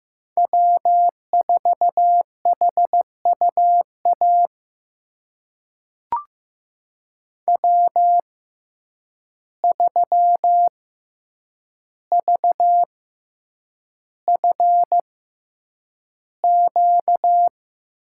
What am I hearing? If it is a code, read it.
Morse code, heard as a keyed tone, about 15 wpm, spaced as for 8 wpm, 700 Hz. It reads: W4HUA W3VFQ
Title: USA Call Signs - 6x Character Spacing - 15wpm